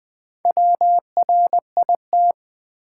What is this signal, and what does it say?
Morse code, heard as a keyed tone, about 20 wpm, 690 Hz. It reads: WRIT
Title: Build A Word - 2 Letters At A Time - 6-Letter Words - 20wpm